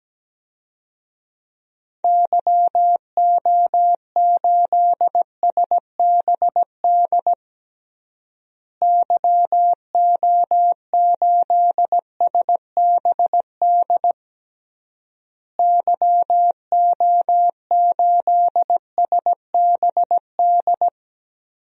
Morse code, heard as a keyed tone, about 17 wpm, 700 Hz. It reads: YO8SBD YO8SBD YO8SBD
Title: DXCC-100 Call Signs - 6 Characters - Repeated 3x - 17wpm